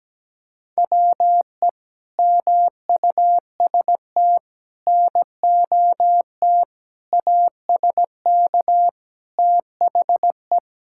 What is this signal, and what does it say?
Morse code, heard as a keyed tone, about 17 wpm, 700 Hz. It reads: WE MUST NOT ASK THE
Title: Sentences from Top 1600 Words 17wpm